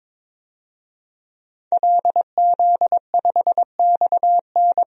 Morse code, heard as a keyed tone, about 22 wpm, 700 Hz. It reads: LZ5XN
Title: DXCC-100 Call Signs - 5 Characters - 22wpm